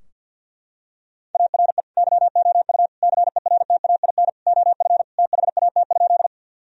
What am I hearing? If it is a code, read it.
Morse code, heard as a keyed tone, about 52 wpm, 700 Hz. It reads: ARE YOU CERTAIN OF THAT?
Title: Sentences from Top 1000 Words - Review Part 2 - 50wpm